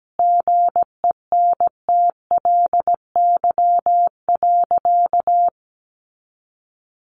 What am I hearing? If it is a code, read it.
Morse code, heard as a keyed tone, about 17 wpm, 700 Hz. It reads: GENTLY.
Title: Sentences from Top 1500 Words 17wpm